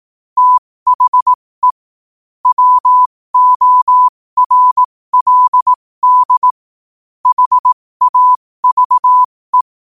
Morse code, heard as a keyed tone, about 18 wpm, spaced as for 15 wpm, 1000 Hz. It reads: THE WORLD HAVE
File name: qotdfiles / 15wpm